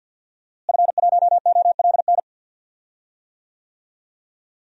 Morse code, heard as a keyed tone, about 48 wpm, 700 Hz. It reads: F1OLN
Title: DXCC-100 Call Signs - 5 Characters - Repeated 3x - 50wpm